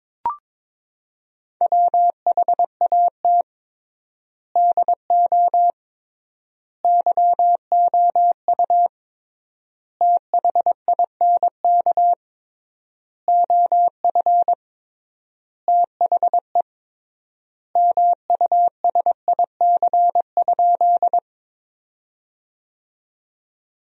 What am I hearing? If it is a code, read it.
Morse code, heard as a keyed tone, about 22 wpm, 700 Hz. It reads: WHAT DO YOU THINK OF THE MUSIC?
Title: 3x Word-Spacing - Sentences from Top 300 Words - 22wpm